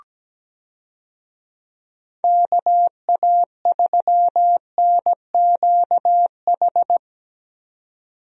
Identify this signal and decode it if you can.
Morse code, heard as a keyed tone, about 17 wpm, 700 Hz. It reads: KA3NQH
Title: USA Call Signs - 6 Characters - 17wpm